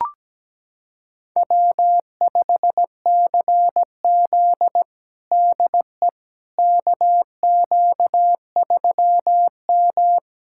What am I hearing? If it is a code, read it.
Morse code, heard as a keyed tone, about 17 wpm, 700 Hz. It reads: W5CZ DE KQ3M
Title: USA Callsign DE Callsign - 4 Characters - 17wpm